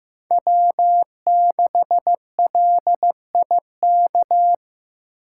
Morse code, heard as a keyed tone, about 15 wpm, 700 Hz. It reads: W6LIK
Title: USA Call Signs - 5 Characters - 6x Character Spacing - 15wpm